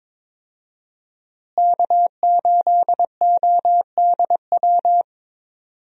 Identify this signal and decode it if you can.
Morse code, heard as a keyed tone, about 22 wpm, 700 Hz. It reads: K8ODW
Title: USA Call Signs - 5 Characters - Repeated 3x - 22wpm